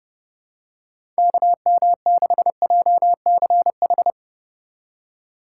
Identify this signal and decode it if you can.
Morse code, heard as a keyed tone, about 30 wpm, 700 Hz. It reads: KM6JCH
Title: USA Call Signs - 6 Characters - 3x Character Spacing - 30wpm